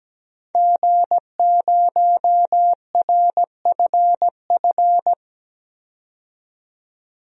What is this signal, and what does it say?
Morse code, heard as a keyed tone, about 17 wpm, 700 Hz. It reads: G0RFF
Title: DXCC-100 Call Signs - 5 Characters - 17wpm